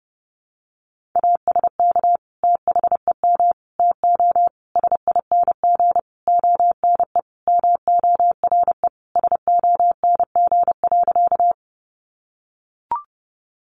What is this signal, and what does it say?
Morse code, heard as a keyed tone, about 30 wpm, 700 Hz. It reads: ASK THEM TO SING ONE MORE SONG.
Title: Sentences from Top 400 Words 30wpm